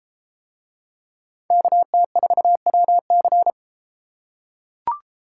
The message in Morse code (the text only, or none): KT4WC